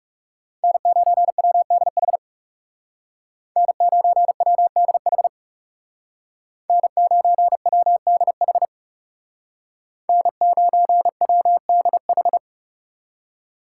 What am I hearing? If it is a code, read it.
Morse code, heard as a keyed tone, about 36 wpm, 700 Hz. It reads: N9WDH N9WDH N9WDH N9WDH
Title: Speed Racing - USA 1x3 Call Signs 30wpm